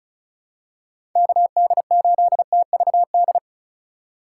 Morse code, heard as a keyed tone, about 36 wpm, 700 Hz. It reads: KD8TVD
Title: USA Call Signs - 6 Characters - 3x Character Spacing - 35wpm